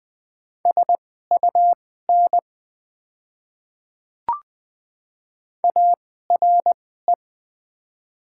Morse code, heard as a keyed tone, about 20 wpm, spaced as for 15 wpm, 690 Hz. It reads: SUN ARE
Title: 2x Character Spacing - Top 500 Words - 20wpm